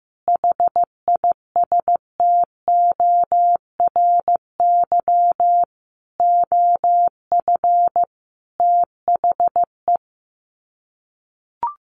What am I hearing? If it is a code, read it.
Morse code, heard as a keyed tone, about 15 wpm, 700 Hz. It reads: HISTORY OF THE
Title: Sets of 5 Words 15wpm